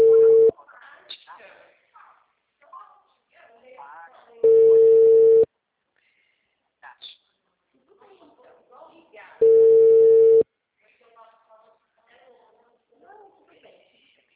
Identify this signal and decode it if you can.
Ringback tone (China, 450 Hz), with speech.